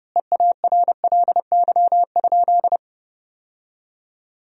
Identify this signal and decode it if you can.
Morse code, heard as a keyed tone, about 30 wpm, 700 Hz. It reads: EARLY?
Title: 2x Word-Spacing - Sentences from Top 400 Words - 30wpm